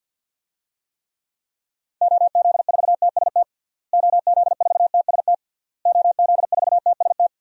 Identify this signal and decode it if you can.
Morse code, heard as a keyed tone, about 52 wpm, 700 Hz. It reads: OZ4TST OZ4TST OZ4TST
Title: DXCC-100 Call Signs - 6 Characters - Repeated 3x - 50wpm